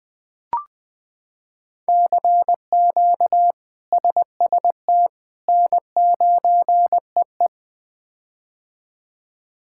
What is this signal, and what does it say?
Morse code, heard as a keyed tone, about 20 wpm, 700 Hz. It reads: CQ SST N9EE
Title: K1USN SST QSO Practice - 20wpm